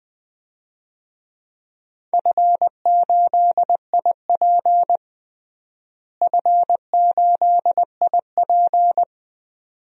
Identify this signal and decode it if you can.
Morse code, heard as a keyed tone, about 20 wpm, 700 Hz. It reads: F8IP F8IP
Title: DXCC-100 Call Signs - 4 Characters - Repeated 2x - 20wpm